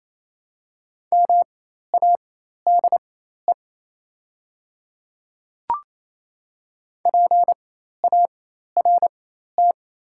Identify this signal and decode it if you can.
Morse code, heard as a keyed tone, about 28 wpm, spaced as for 13 wpm, 700 Hz. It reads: MADE PART
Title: 4x Character Spacing - Top 200 Words - 28wpm